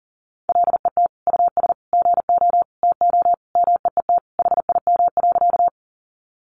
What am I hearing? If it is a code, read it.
Morse code, heard as a keyed tone, about 40 wpm, 700 Hz. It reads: LET US GO TO MEET HIM.
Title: Sentences from Top 1100 Words 40wpm